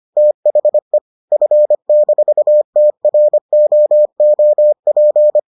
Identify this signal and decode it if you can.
Morse code, heard as a keyed tone, about 25 wpm, 600 Hz. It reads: THE F-TROOP